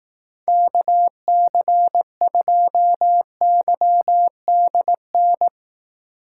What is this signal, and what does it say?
Morse code, heard as a keyed tone, about 18 wpm, 700 Hz. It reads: KC2YDN
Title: Speed Racing - USA 2x3 Call Signs 15wpm